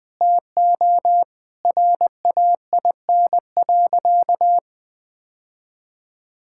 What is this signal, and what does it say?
Morse code, heard as a keyed tone, about 20 wpm, 700 Hz. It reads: TO RAIN.
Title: Sentences with 2-Word Phrases - 20wpm